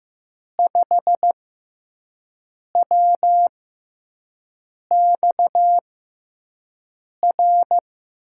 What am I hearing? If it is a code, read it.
Morse code, heard as a keyed tone, about 15 wpm, spaced as for 5 wpm, 700 Hz. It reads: 5WXR
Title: USA Call Signs - 6 Characters - 6x Character Spacing - 15wpm